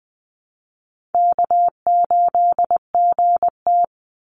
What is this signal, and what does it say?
Morse code, heard as a keyed tone, about 20 wpm, 700 Hz. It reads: K8GT
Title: Call Signs - 20wpm